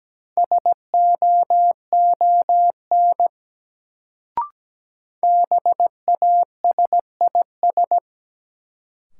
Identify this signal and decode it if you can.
Morse code, heard as a keyed tone, about 17 wpm, 700 Hz. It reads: SOON BASIS
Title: Sets of 3 Words - New Word Drill - 17wpm